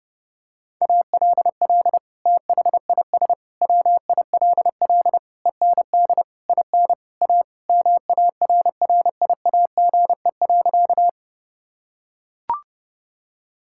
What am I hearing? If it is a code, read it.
Morse code, heard as a keyed tone, about 32 wpm, 700 Hz. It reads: ALL THIS WILL END IN A MARRIAGE.
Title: Sentences from Top 1000 Words - Review Part 5 - 30wpm